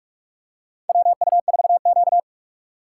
Morse code, heard as a keyed tone, about 46 wpm, 700 Hz. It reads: WU4Q
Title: USA Call Signs - 4 Characters - 6x Character Spacing - 45wpm